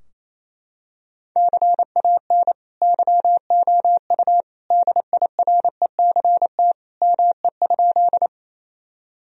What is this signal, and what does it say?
Morse code, heard as a keyed tone, about 27 wpm, 700 Hz. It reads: CAN YOU DIRECT ME?
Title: Sentences from Top 1000 Words - Review Part 2 - 28wpm